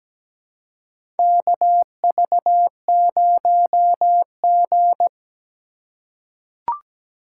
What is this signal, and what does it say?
Morse code, heard as a keyed tone, about 17 wpm, 700 Hz. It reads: KV0G
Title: USA Call Signs - 4 Characters - 3x Character Spacing - 17wpm